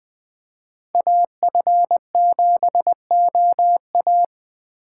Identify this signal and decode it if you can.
Morse code, heard as a keyed tone, about 20 wpm, 700 Hz. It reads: AF7OA
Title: Speed Racing - USA Call Signs 17wpm